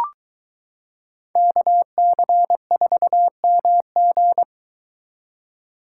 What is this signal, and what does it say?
Morse code, heard as a keyed tone, about 23 wpm, 700 Hz. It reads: KC4MG